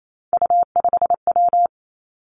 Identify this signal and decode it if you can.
Morse code, heard as a keyed tone, about 28 wpm, 700 Hz. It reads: U5W
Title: Three Challenging Letter-Number Combinations 28wpm